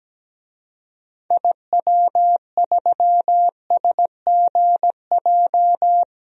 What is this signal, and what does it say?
Morse code, heard as a keyed tone, about 17 wpm, 700 Hz. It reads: IW3SGJ
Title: DXCC-100 Call Signs - 6 Characters - 17wpm